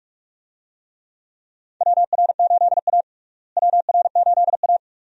Morse code, heard as a keyed tone, about 46 wpm, 700 Hz. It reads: WR8A WR8A
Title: USA Call Signs - 4 Characters - Repeated 2x - 45wpm